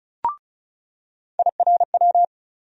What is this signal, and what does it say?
Morse code, heard as a keyed tone, about 34 wpm, 700 Hz. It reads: IRW
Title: USA Call Sign Postfixes - 3 Letters 35wpm